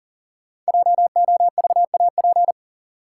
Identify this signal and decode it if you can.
Morse code, heard as a keyed tone, about 37 wpm, 700 Hz. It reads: JOVAP
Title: Code Groups - 5 Letters - 40wpm